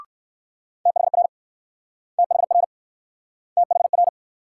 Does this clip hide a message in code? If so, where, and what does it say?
Yes: Morse code from 0.9 to 4.1 s, heard as a keyed tone, about 60 wpm, 700 Hz. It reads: THR THR THR